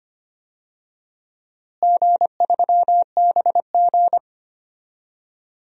Morse code, heard as a keyed tone, about 25 wpm, 700 Hz. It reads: G3BG
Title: DXCC-100 Call Signs - 4 Characters - Repeated 2x - 25wpm